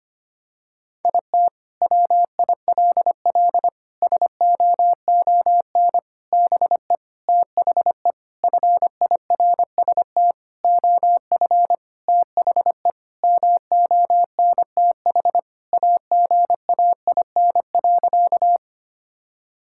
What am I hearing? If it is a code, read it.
Morse code, heard as a keyed tone, about 25 wpm, 700 Hz. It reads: IT WILL SOON BE THE FIRST OF THE MONTH AGAIN.
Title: Sentences from Top 1000 Words - Review Part 5 - 25wpm